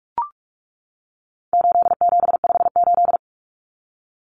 Morse code, heard as a keyed tone, about 44 wpm, 700 Hz. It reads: 8758
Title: Numbers - 4-Digit 45wpm